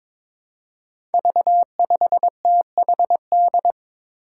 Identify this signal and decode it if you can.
Morse code, heard as a keyed tone, about 21 wpm, 700 Hz. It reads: V5THD
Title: Five Challenging Letter-Number Combinations 22wpm